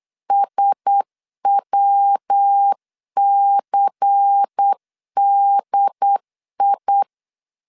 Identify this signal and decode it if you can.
Morse code, heard as a keyed tone, about 8 wpm, 800 Hz. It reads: SWCDI
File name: LEZIONI / 6 PKX0 / 6 PKX 8wpm 3041